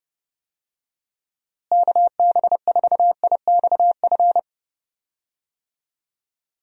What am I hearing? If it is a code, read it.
Morse code, heard as a keyed tone, about 30 wpm, 700 Hz. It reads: KB4IXF
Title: USA Call Signs - 6 Characters - Repeated 3x - 30wpm